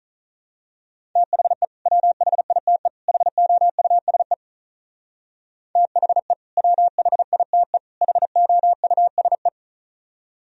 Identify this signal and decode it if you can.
Morse code, heard as a keyed tone, about 38 wpm, 700 Hz. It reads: THE WHITE HOUSE THE WHITE HOUSE
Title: Speed-Racing - Sets of 3 Words 35wpm